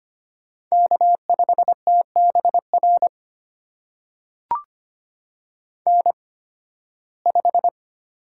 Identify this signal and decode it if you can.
Morse code, heard as a keyed tone, about 25 wpm, spaced as for 16 wpm, 700 Hz. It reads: K5TBR N5
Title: USA Call Signs - 8x Character Spacing - 25wpm